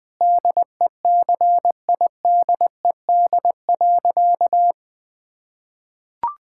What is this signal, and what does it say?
Morse code, heard as a keyed tone, about 20 wpm, 700 Hz. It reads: DECIDED.
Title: Sentences with 3-Word Phrases - 20wpm